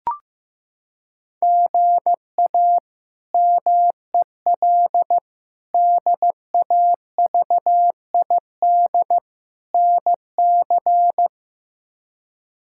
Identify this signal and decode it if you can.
Morse code, heard as a keyed tone, about 15 wpm, 700 Hz. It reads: GA MEL DAVID NC
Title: K1USN SST QSO Practice - 15wpm